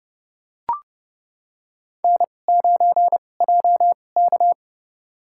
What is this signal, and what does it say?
Morse code, heard as a keyed tone, about 30 wpm, spaced as for 22 wpm, 700 Hz. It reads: N9JK
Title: USA Call Signs - 4 Characters - 2x Character Spacing - 30wpm